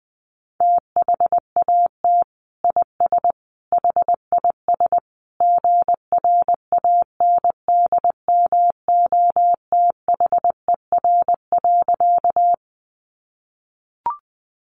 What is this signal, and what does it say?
Morse code, heard as a keyed tone, about 20 wpm, 700 Hz. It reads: THAT IS HIS GRANDMOTHER.
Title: Sentences from Top 1200 Words 20wpm